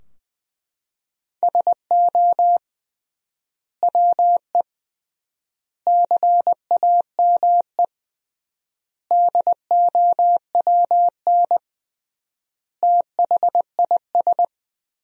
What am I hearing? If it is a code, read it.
Morse code, heard as a keyed tone, about 20 wpm, 700 Hz. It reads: SO WE CAME DOWN THIS
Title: 3x Word-Spacing - Sentences from Top 400 Words - 20wpm